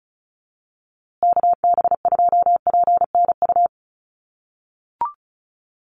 Morse code, heard as a keyed tone, about 34 wpm, 700 Hz. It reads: KB2PNU
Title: USA Call Signs Repeated 2x 35wpm